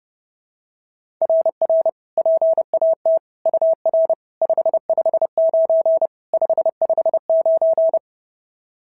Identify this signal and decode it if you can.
Morse code, heard as a keyed tone, about 30 wpm, 690 Hz. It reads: RR PAT UR 559 559
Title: SOTA QSO Practice 30wpm